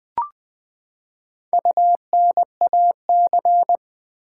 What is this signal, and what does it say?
Morse code, heard as a keyed tone, about 21 wpm, 700 Hz. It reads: UNAC
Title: Build A Word - 4 Letters At A Time - 12-Letter Words - 20wpm